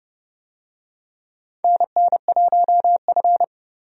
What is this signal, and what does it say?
Morse code, heard as a keyed tone, about 32 wpm, 700 Hz. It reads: NN1F
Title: USA Call Signs - 4 Characters - Repeated 2x - 30wpm